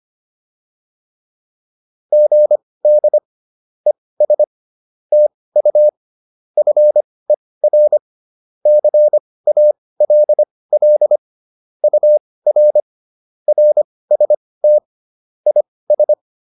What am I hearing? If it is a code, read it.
Morse code, heard as a keyed tone, about 25 wpm, spaced as for 18 wpm, 600 Hz. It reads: GD ES TU FER CALL UR RST IS
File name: QSO107 18